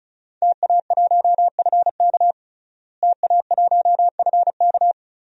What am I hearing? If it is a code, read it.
Morse code, heard as a keyed tone, about 34 wpm, 700 Hz. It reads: TA1FK TA1FK
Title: DXCC-100 Call Signs - 5 Characters - Repeated 3x - 35wpm